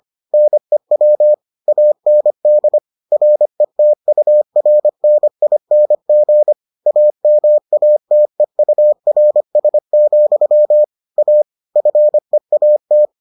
Morse code, heard as a keyed tone, about 25 wpm, 600 Hz. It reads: NEW AND RETURNING AMATEURS, A FEAT